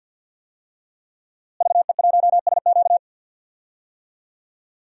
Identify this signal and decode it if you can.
Morse code, heard as a keyed tone, about 50 wpm, 700 Hz. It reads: VE1SQ
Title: DXCC-100 Call Signs - 5 Characters - Repeated 2x - 50wpm